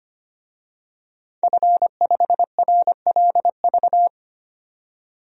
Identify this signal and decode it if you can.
Morse code, heard as a keyed tone, about 25 wpm, 700 Hz. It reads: F5RLV
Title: DXCC-100 Call Signs - 5 Characters - 25wpm